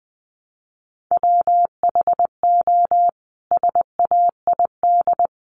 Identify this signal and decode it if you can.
Morse code, heard as a keyed tone, about 20 wpm, 700 Hz. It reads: WHO SAID
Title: Sentences from Top 300 Words 20wpm